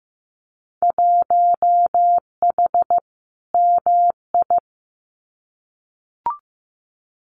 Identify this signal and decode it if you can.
Morse code, heard as a keyed tone, about 15 wpm, 700 Hz. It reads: 1H MI
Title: Winter Field Day Exchanges 15wpm